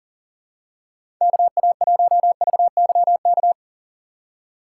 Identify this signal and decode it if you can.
Morse code, heard as a keyed tone, about 40 wpm, 700 Hz. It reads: KA1VYK
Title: USA Call Signs - 6 Characters - 6x Character Spacing - 40wpm